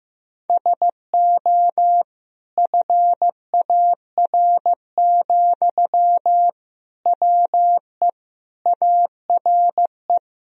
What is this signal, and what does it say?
Morse code, heard as a keyed tone, about 15 wpm, 700 Hz. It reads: SO FAR, WE ARE
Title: Sentences from Top 1000 Words - Review Part 4 - 15wpm